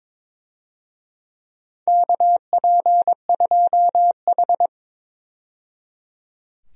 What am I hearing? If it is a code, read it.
Morse code, heard as a keyed tone, about 22 wpm, 700 Hz. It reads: KP2H
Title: DXCC-100 Call Signs - 4 Characters - 22wpm